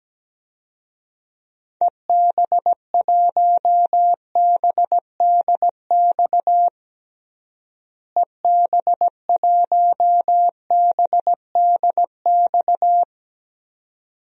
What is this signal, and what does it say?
Morse code, heard as a keyed tone, about 17 wpm, 700 Hz. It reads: EB1BDX EB1BDX
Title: DXCC-100 Call Signs - 6 Characters - Repeated 2x - 17wpm